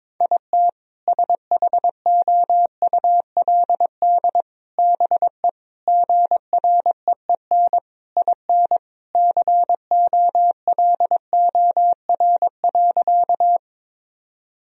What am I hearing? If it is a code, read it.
Morse code, heard as a keyed tone, about 22 wpm, 700 Hz. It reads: IT SHOULD BE GREEN IN COLOR.
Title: Sentences from Top 1000 Words - Review Part 2 - 22wpm